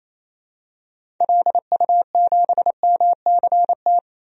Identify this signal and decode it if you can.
Morse code, heard as a keyed tone, about 28 wpm, 700 Hz. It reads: LU7MCT
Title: DXCC-100 Call Signs - 6 Characters - 28wpm